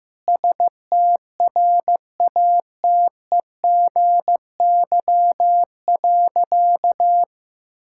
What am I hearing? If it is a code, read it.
Morse code, heard as a keyed tone, about 15 wpm, 700 Hz. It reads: STRATEGY.
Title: Sentences from Top 1000 Words - Review Part 6 - 15wpm